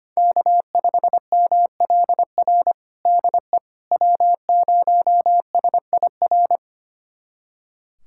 Morse code, heard as a keyed tone, about 25 wpm, 700 Hz. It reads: K5MLR DE W0SIR
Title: USA Callsign DE Callsign - 5 Characters - 25wpm